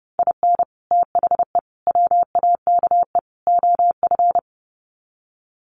Morse code, heard as a keyed tone, about 30 wpm, 700 Hz. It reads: IN THE WAKE OF
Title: Sets of 4 Words 30wpm